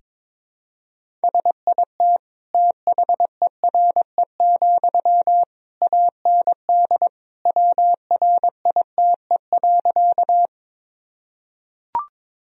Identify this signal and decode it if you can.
Morse code, heard as a keyed tone, about 22 wpm, 700 Hz. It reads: SIT THERE, AND WRITE.
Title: Sentences from Top 1000 Words - Review Part 1 - 22wpm